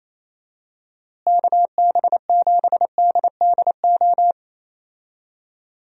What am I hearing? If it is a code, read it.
Morse code, heard as a keyed tone, about 28 wpm, 700 Hz. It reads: KB7DDO
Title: USA Call Signs - 6 Characters - Repeated 2x - 28wpm